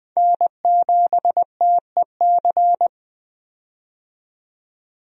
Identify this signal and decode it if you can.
Morse code, heard as a keyed tone, about 20 wpm, 700 Hz. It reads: N7TEC
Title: USA Call Signs - 5 Characters - 20wpm